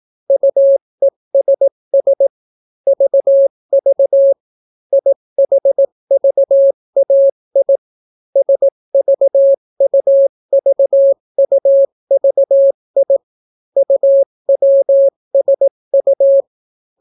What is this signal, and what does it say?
Morse code, heard as a keyed tone, about 18 wpm, 550 Hz. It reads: UESS VV IHVAI SVUVUVI UWSU